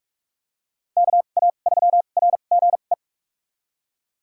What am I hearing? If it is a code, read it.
Morse code, heard as a keyed tone, about 46 wpm, spaced as for 32 wpm, 700 Hz. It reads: KA3RGE
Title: USA Call Signs - 2x Character Spacing - 45wpm